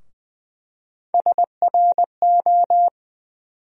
Morse code, heard as a keyed tone, about 20 wpm, 700 Hz. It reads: SRO
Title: USA Call Sign Postfixes - 3 Letters 20wpm